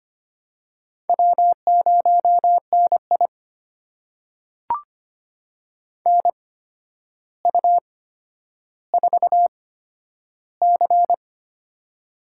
Morse code, heard as a keyed tone, about 25 wpm, spaced as for 11 wpm, 700 Hz. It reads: W0NI NU4C
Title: USA Call Signs - 4 Characters - 8x Character Spacing - 25wpm